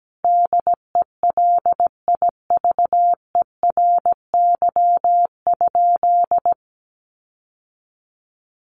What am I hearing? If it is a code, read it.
Morse code, heard as a keyed tone, about 17 wpm, 700 Hz. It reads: DELIVERY?
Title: Sentences from Top 1500 Words 17wpm